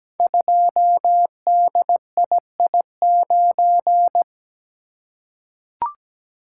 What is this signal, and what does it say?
Morse code, heard as a keyed tone, about 17 wpm, 700 Hz. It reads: 2DII9
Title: Code Groups - 5 Letter:Numbers - 17wpm